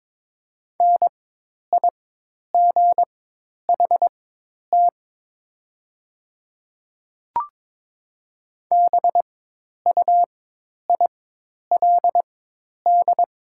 Morse code, heard as a keyed tone, about 22 wpm, spaced as for 10 wpm, 700 Hz. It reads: NIGHT BUILD
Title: 4x Character Spacing - Top 500 Words - 22wpm